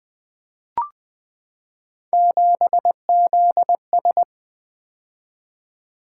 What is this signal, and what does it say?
Morse code, heard as a keyed tone, about 20 wpm, 700 Hz. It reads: 7ZS
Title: Three Letter-Number Combinations - 20wpm